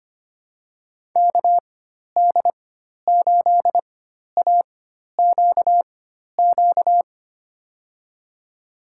Morse code, heard as a keyed tone, about 25 wpm, spaced as for 12 wpm, 700 Hz. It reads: KD8AQQ
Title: USA Call Signs - 6 Characters - 4x Character Spacing - 25wpm